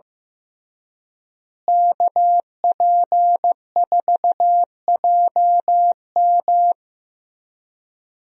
Morse code, heard as a keyed tone, about 15 wpm, 700 Hz. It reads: KP4JM
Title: DXCC-100 Call Signs - 5 Characters - Repeated 3x - 15wpm